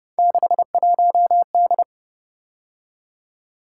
Morse code, heard as a keyed tone, about 29 wpm, 700 Hz. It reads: 61D